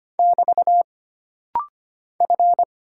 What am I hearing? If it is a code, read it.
Morse code, heard as a keyed tone, about 25 wpm, 700 Hz. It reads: <BT> F
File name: Single Character (Letter:Number:Punctuation:Prosign) - Rapid-Fire - 25wpm